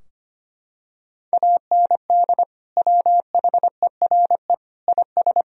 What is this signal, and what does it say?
Morse code, heard as a keyed tone, about 25 wpm, 700 Hz. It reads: AND WHERE IS
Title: Sentences from Top 1000 Words - Review Part 4 - 25wpm